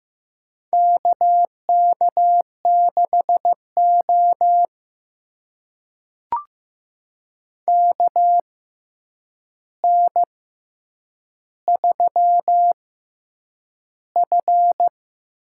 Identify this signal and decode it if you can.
Morse code, heard as a keyed tone, about 15 wpm, spaced as for 8 wpm, 700 Hz. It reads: KK6O KN3F
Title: USA Call Signs - 4 Characters - 6x Character Spacing - 15wpm